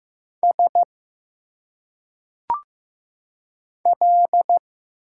Morse code, heard as a keyed tone, about 15 wpm, 700 Hz. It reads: S L
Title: Single Letters - Phonetic Pronunciation 15wpm